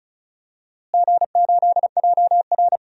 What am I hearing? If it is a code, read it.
Morse code, heard as a keyed tone, about 36 wpm, 700 Hz. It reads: G8JR
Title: DXCC-100 Call Signs - 4 Characters - Repeated 2x - 35wpm